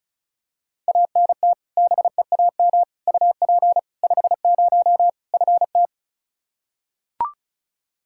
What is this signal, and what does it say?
Morse code, heard as a keyed tone, about 36 wpm, 700 Hz. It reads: ANT BEAM UP 50 FT
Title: K1USN SST QSO Practice 35wpm